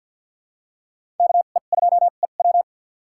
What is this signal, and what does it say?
Morse code, heard as a keyed tone, about 51 wpm, spaced as for 36 wpm, 700 Hz. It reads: KE2EW
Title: USA Call Signs - 2x Character Spacing - 50wpm